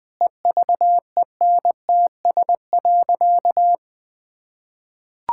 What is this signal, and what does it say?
Morse code, heard as a keyed tone, about 20 wpm, 700 Hz. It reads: EVENTS.
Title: Sentences with 6-Word Phrases - 20wpm